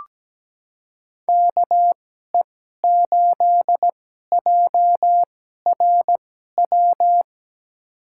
Morse code, heard as a keyed tone, about 17 wpm, spaced as for 12 wpm, 700 Hz. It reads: KE8JRW
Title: USA Call Signs - 2x Character Spacing - 17wpm